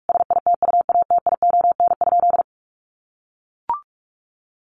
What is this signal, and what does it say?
Morse code, heard as a keyed tone, about 44 wpm, 700 Hz. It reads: SITUATION?
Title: Sentences from Top 1200 Words 45wpm